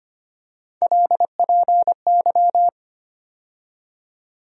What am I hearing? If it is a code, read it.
Morse code, heard as a keyed tone, about 25 wpm, 700 Hz. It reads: LPY
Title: USA Call Sign Postfixes - 3 Letters 25wpm